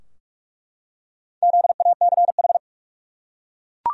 Morse code, heard as a keyed tone, about 41 wpm, 700 Hz. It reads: ZACH
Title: USA Common Names - 45wpm